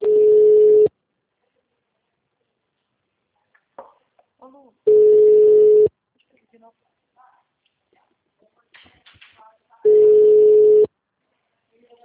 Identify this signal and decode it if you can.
Ringback tone (Europe, 425 Hz), with speech.